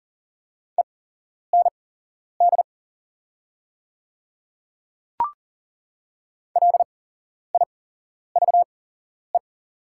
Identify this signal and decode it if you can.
Morse code, heard as a keyed tone, about 40 wpm, spaced as for 11 wpm, 690 Hz. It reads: END LIVE
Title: 8x Character Spacing - Top 200 Words - 40wpm